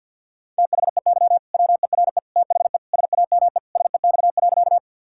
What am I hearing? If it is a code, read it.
Morse code, heard as a keyed tone, about 51 wpm, 700 Hz. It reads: THEY WERE THE SAME SEX.
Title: Sentences from Top 1600 Words 50wpm